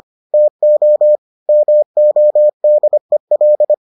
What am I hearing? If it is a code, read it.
Morse code, heard as a keyed tone, about 25 wpm, 600 Hz. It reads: TO MODEL